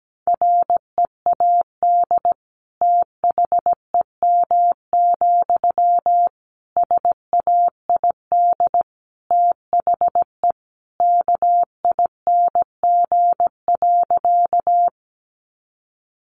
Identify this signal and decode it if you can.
Morse code, heard as a keyed tone, about 17 wpm, 700 Hz. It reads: READ THEM, SAID THE KING.
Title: Sentences from Top 400 Words 17wpm